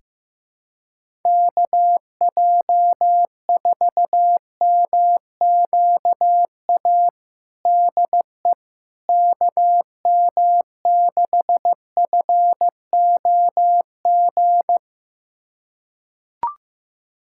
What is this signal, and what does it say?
Morse code, heard as a keyed tone, about 15 wpm, 700 Hz. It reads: KJ4MQA DE KM6FOG
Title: USA Callsign DE Callsign - 6 Characters - 15wpm